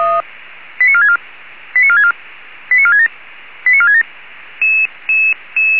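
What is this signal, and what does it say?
Five-tone selective call (ZVEI), heard as quick runs of five tones, tones of about 70 ms. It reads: 89464 89464 89467 89467
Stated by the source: Received with an SDR as ZVEI five-tone selective call, code 89346.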